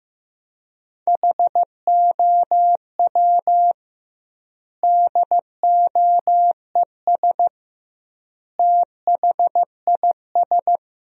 Morse code, heard as a keyed tone, about 15 wpm, 700 Hz. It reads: HOW DOES THIS
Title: 2x Word-Spacing - Sentences from Top 400 Words - 15wpm